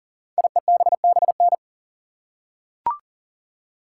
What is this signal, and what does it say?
Morse code, heard as a keyed tone, about 39 wpm, 700 Hz. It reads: IEBBN
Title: Code Groups - 5 Letters - 40wpm